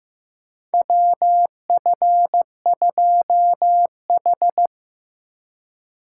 Morse code, heard as a keyed tone, about 15 wpm, 700 Hz. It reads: WF2H
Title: USA Call Signs - 4 Characters - 3x Character Spacing - 15wpm